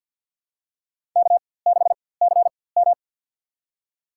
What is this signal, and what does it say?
Morse code, heard as a keyed tone, about 52 wpm, spaced as for 23 wpm, 700 Hz. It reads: K6CM
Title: USA Call Signs - 4x Character Spacing - 50wpm